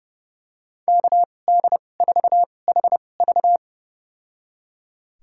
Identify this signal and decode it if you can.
Morse code, heard as a keyed tone, about 30 wpm, spaced as for 22 wpm, 700 Hz. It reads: KD4HV
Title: USA Call Signs - 5 Characters - 2x Character Spacing - 30wpm